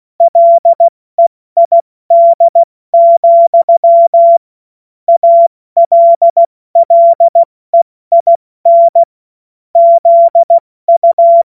Morse code, heard as a keyed tone, about 16 wpm, 680 Hz. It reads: LEID, ALLEIN ZU
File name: Morsekurs / Witze / Witz 24.Oktober 2023 70 F80